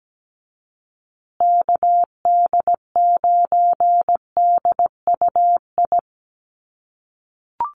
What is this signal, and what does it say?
Morse code, heard as a keyed tone, about 17 wpm, 700 Hz. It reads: KD9DUI
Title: USA Call Signs Repeated 3x 17wpm